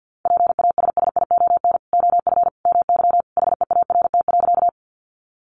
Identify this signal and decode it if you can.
Morse code, heard as a keyed tone, about 50 wpm, 700 Hz. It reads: PASSION OF MY HEART.
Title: Sentences from Top 1300 Words 50wpm